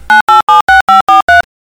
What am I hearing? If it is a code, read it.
DTMF, heard as telephone touch tones, tones of about 150 ms. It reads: 9876543